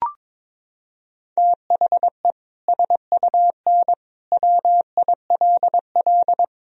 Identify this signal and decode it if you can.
Morse code, heard as a keyed tone, about 22 wpm, 700 Hz. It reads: THE SUN WILL
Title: Sentences from Top 1000 Words - Review Part 3 - 22wpm